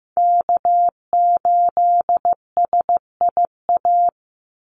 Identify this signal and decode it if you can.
Morse code, heard as a keyed tone, about 15 wpm, 700 Hz. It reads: K8SIA
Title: Call Signs Repeated 3x - 15wpm